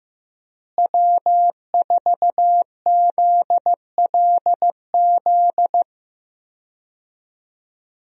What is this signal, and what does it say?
Morse code, heard as a keyed tone, about 15 wpm, 700 Hz. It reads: W4ZLZ
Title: USA Call Signs - 5 Characters - Repeated 3x - 15wpm